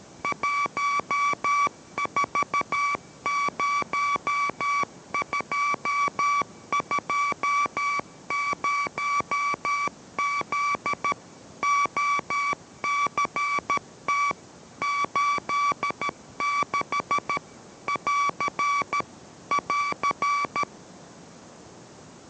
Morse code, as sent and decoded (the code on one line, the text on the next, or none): .---- ....- ----- ..--- ..--- ----- --.. / --- -.-. - / ---.. -.... / .-.-. / .-.-.
140220Z OCT 86 <AR> <AR>